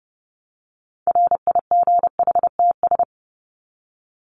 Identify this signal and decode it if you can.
Morse code, heard as a keyed tone, about 30 wpm, 700 Hz. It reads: RIGHTS
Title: ES:S Suffix - Top 100 Words with Base Words 30wpm